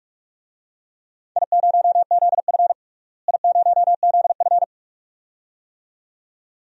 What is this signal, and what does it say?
Morse code, heard as a keyed tone, about 42 wpm, 700 Hz. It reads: I0ZF I0ZF
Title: DXCC-100 Call Signs - 4 Characters - Repeated 2x - 45wpm